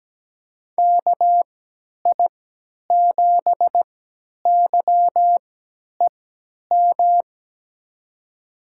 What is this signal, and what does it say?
Morse code, heard as a keyed tone, about 17 wpm, spaced as for 10 wpm, 700 Hz. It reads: KI7YEM
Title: USA Call Signs - 6 Characters - 3x Character Spacing - 17wpm